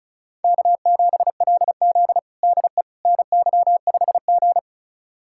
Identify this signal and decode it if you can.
Morse code, heard as a keyed tone, about 36 wpm, 700 Hz. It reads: K7LZ DE NY5G
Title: USA Callsign DE Callsign - 4 Characters - 35wpm